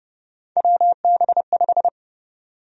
Morse code, heard as a keyed tone, about 32 wpm, 700 Hz. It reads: WB5
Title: USA Call Sign Prefixes - 30wpm